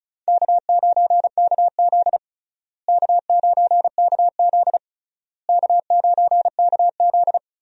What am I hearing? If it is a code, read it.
Morse code, heard as a keyed tone, about 36 wpm, 700 Hz. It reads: K9KZ K9KZ K9KZ
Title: USA Call Signs - 4 Characters - Repeated 3x - 35wpm